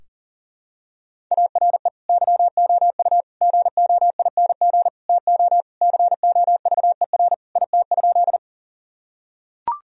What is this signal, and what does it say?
Morse code, heard as a keyed tone, about 40 wpm, 700 Hz. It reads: ARE YOU GOING TO COVER IT?
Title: Sentences from Top 1000 Words - Review Part 1 - 40wpm